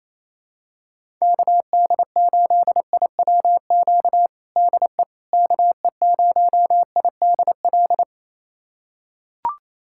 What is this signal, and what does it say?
Morse code, heard as a keyed tone, about 27 wpm, 700 Hz. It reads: KD8IWQ DE KE0IDL
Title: USA Callsign DE Callsign - 6 Characters - 28wpm